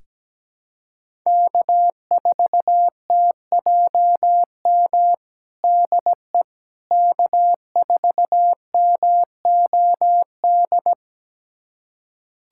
Morse code, heard as a keyed tone, about 17 wpm, 700 Hz. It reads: K4TJM DE K4MOD
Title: USA Callsign DE Callsign - 5 Characters - 17wpm